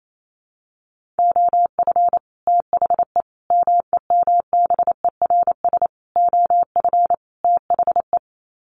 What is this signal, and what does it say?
Morse code, heard as a keyed tone, about 28 wpm, 700 Hz. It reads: OF THE MEMBERS OF THE
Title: Sets of 5 Words Encore2 28wpm